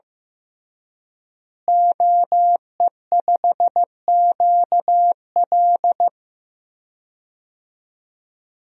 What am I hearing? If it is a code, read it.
Morse code, heard as a keyed tone, about 15 wpm, 700 Hz. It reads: OE5QL
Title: DXCC-100 Call Signs - 5 Characters - Repeated 2x - 15wpm